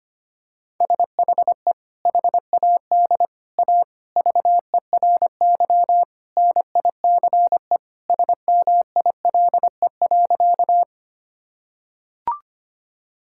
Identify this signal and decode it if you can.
Morse code, heard as a keyed tone, about 25 wpm, 700 Hz. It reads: SHE HAD A VERY NICE SMILE.
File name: practice / 25wpm / Sentences with 2-Word Phrases - 25wpm